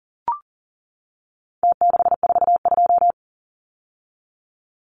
Morse code, heard as a keyed tone, about 40 wpm, 700 Hz. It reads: T642
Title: Numbers - Contest Style - 4-Digit 40wpm